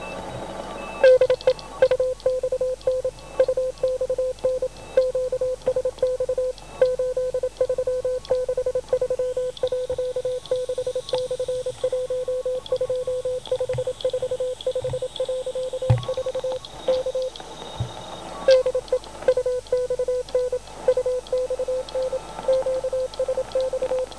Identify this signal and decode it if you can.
Morse code, heard as a keyed tone, about 27 wpm, 510 Hz. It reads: DE UXN UXN QSX 8363.6/12545.4 K DE UXN UXN QSX